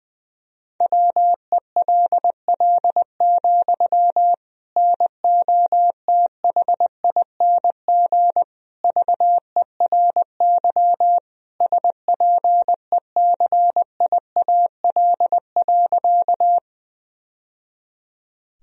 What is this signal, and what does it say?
Morse code, heard as a keyed tone, about 20 wpm, 700 Hz. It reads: WELL, NOTHING VERY SPECIAL.
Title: Sentences from Top 1000 Words - Review Part 3 - 20wpm